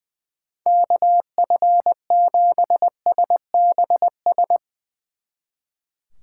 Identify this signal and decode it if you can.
Morse code, heard as a keyed tone, about 24 wpm, 700 Hz. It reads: KF7SBS
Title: Speed Racing - USA 2x3 Call Signs 20wpm